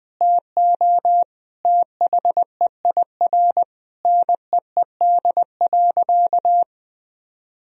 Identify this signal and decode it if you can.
Morse code, heard as a keyed tone, about 21 wpm, 700 Hz. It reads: TO THEIR NEED.
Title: Sentences from Top 1000 Words - Review Part 4 - 20wpm